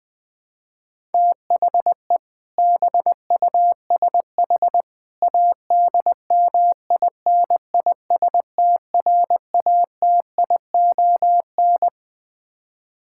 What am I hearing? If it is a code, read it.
Morse code, heard as a keyed tone, about 20 wpm, 700 Hz. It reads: THE BUSH ADMINISTRATION
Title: Build A Phrase - 3-Word Phrases - 20wpm